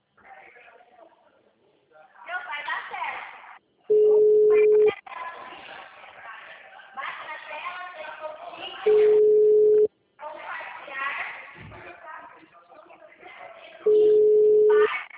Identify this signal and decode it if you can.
Ringback tone (Europe, 425 Hz), with speech.